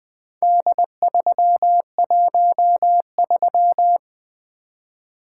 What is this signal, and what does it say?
Morse code, heard as a keyed tone, about 20 wpm, 700 Hz. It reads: D313
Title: Four Letter-Number Combinations - 20wpm